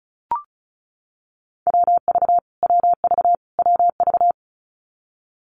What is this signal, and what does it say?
Morse code, heard as a keyed tone, about 34 wpm, 700 Hz. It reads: WV WV WV
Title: US States Repeated 3x 35wpm